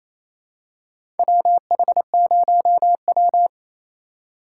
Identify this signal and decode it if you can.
Morse code, heard as a keyed tone, about 28 wpm, 700 Hz. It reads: WH0W
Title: USA Call Signs - 4x Character Spacing - 28wpm